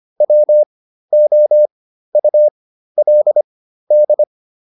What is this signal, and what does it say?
Morse code, heard as a keyed tone, about 25 wpm, spaced as for 13 wpm, 600 Hz. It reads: WOULD